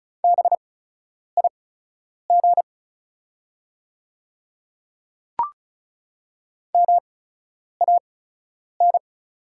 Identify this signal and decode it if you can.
Morse code, heard as a keyed tone, about 36 wpm, spaced as for 10 wpm, 700 Hz. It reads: BIG MAN
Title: 8x Character Spacing - Top 200 Words - 35wpm